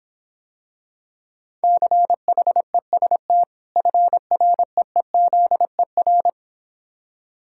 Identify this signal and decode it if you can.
Morse code, heard as a keyed tone, about 26 wpm, 700 Hz. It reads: CHEST FREEZER